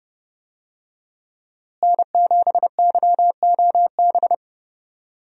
Morse code, heard as a keyed tone, about 30 wpm, 700 Hz. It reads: N7YOB
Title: USA Call Signs - 5 Characters - Repeated 2x - 30wpm